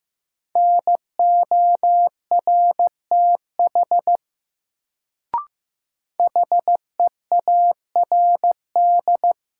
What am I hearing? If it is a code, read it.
Morse code, heard as a keyed tone, about 15 wpm, 700 Hz. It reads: NORTH HEARD
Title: Sentences from Top 400 Words - New Word Drill - 15wpm